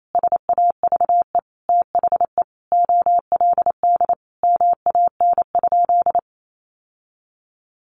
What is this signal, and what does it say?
Morse code, heard as a keyed tone, about 28 wpm, 700 Hz. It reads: SAVE THE OLD MAN?
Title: Sentences from Top 1100 Words 28wpm